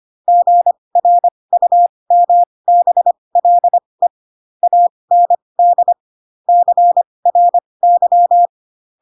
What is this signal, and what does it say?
Morse code, heard as a keyed tone, about 25 wpm, spaced as for 20 wpm, 700 Hz. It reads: GRUMBLE AND CRY
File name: Pinocchio / Pinocchio ch8 15 25 ik1hge